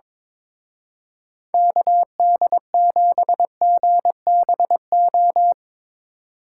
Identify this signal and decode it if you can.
Morse code, heard as a keyed tone, about 24 wpm, 700 Hz. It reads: KD7GBO
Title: Speed Racing - USA 2x3 Call Signs 22wpm